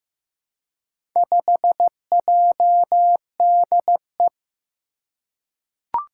Morse code, heard as a keyed tone, about 15 wpm, 700 Hz. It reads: 5JDE
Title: Four Challenging Letter-Number Combinations 15wpm